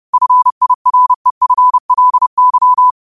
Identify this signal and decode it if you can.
Morse code, heard as a keyed tone, about 30 wpm, 1000 Hz. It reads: FIREFLY